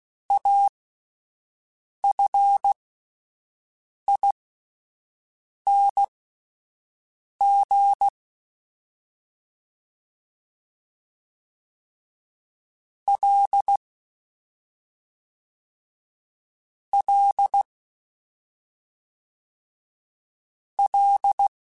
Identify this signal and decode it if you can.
Morse code, heard as a keyed tone, about 16 wpm, spaced as for 5 wpm, 800 Hz. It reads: AFING L L L